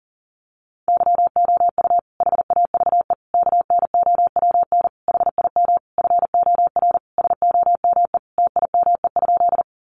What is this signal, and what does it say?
Morse code, heard as a keyed tone, about 40 wpm, 700 Hz. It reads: YOU HAVE KNOWN HIM FOR SOME TIME?
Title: Sentences from Top 600 Words 40wpm